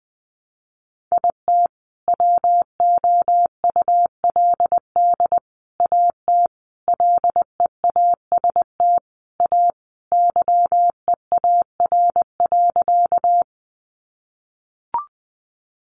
Morse code, heard as a keyed tone, about 20 wpm, 700 Hz. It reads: IT WOULD AT LEAST A YEAR.
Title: Sentences from Top 700 Words 20wpm